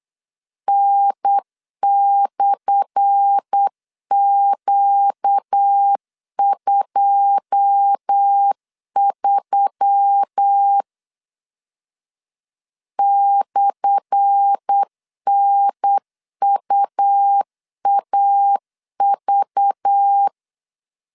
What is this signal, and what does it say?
Morse code, heard as a keyed tone, about 8 wpm, 800 Hz. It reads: N/Q23 /NUAV
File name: LEZIONI / 10 SIGNS / 10A SLASH 8wpm 3452